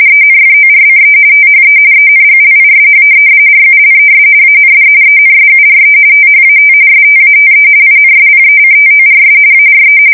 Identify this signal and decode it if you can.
RTTY, about 45.45 baud. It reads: ENFORCE THE ENGLISH RULES AND ENSURE THAT STATE INSPECTORS ARE
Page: 170 Shift 2125/2295